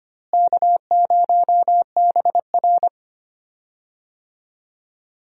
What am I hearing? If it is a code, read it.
Morse code, heard as a keyed tone, about 25 wpm, 700 Hz. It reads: K0BR
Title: USA Call Signs - 4 Characters - Repeated 2x - 25wpm